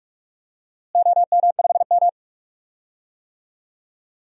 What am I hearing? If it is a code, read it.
Morse code, heard as a keyed tone, about 44 wpm, 700 Hz. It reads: OM5M
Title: DXCC-100 Call Signs - 4 Characters - Repeated 2x - 45wpm